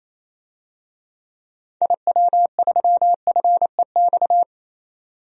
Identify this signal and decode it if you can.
Morse code, heard as a keyed tone, about 28 wpm, 700 Hz. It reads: IW3FEX
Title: DXCC-100 Call Signs - 6 Characters - 28wpm